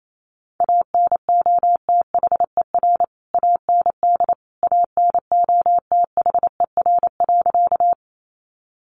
Morse code, heard as a keyed tone, about 28 wpm, 700 Hz. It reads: ANOTHER AND ANOTHER.
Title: Sentences from Top 600 Words 28wpm